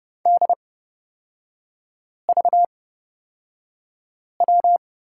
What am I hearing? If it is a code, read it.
Morse code, heard as a keyed tone, about 30 wpm, spaced as for 5 wpm, 700 Hz. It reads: DVW
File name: Lesson 37e - Letter x - Three Characters 30wpm